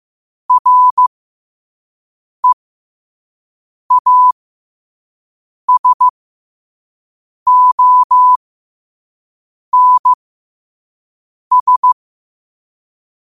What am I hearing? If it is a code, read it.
Morse code, heard as a keyed tone, about 15 wpm, spaced as for 5 wpm, 1000 Hz. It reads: REASONS
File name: qotdfiles / 5wpm